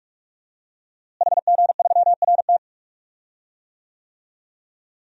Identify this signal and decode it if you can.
Morse code, heard as a keyed tone, about 47 wpm, 700 Hz. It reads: HG3RT
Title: DXCC-100 Call Signs - 5 Characters - Repeated 3x - 45wpm